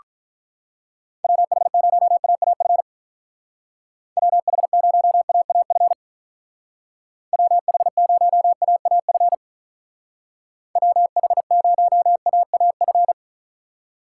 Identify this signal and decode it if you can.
Morse code, heard as a keyed tone, about 43 wpm, 700 Hz. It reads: WH0AAF WH0AAF WH0AAF WH0AAF
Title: Speed Racing - USA Call Signs 35wpm